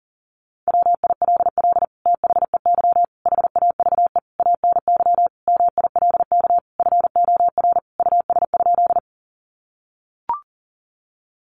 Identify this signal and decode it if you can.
Morse code, heard as a keyed tone, about 40 wpm, 700 Hz. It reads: WILL THEY HAVE ANY MILK FOR US?
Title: Sentences from Top 1500 Words 40wpm